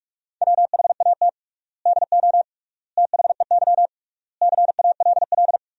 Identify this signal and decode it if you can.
Morse code, heard as a keyed tone, about 45 wpm, 700 Hz. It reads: WHAT DO THEY CALL
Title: 3x Word-Spacing - Sentences from Top 100 Words - 45wpm